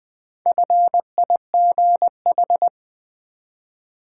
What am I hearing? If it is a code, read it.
Morse code, heard as a keyed tone, about 20 wpm, 700 Hz. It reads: FIGH
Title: Build A Word - 1 Letter At A Time - 3 to 5-Letter Words - 20wpm